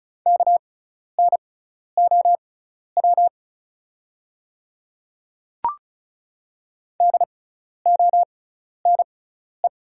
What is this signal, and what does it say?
Morse code, heard as a keyed tone, about 36 wpm, spaced as for 12 wpm, 690 Hz. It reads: KNOW DONE
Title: 6x Character Spacing - Top 500 Words - 35wpm